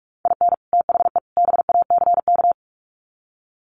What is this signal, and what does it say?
Morse code, heard as a keyed tone, about 44 wpm, 700 Hz. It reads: IN THE BACK
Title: Sets of 3 Words 45wpm